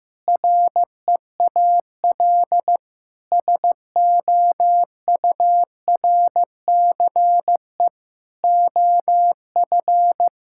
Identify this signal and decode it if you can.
Morse code, heard as a keyed tone, about 15 wpm, 700 Hz. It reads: REAL SOURCE OF
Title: Sentences from Top 1000 Words - Review Part 5 - 15wpm